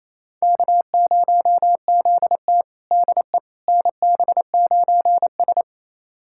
Morse code, heard as a keyed tone, about 28 wpm, 700 Hz. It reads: K0ZT DE NB9S
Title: USA Callsign DE Callsign - 4 Characters - 28wpm